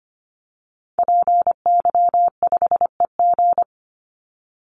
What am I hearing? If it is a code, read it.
Morse code, heard as a keyed tone, about 25 wpm, 700 Hz. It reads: PY5EG
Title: Call Signs - 25wpm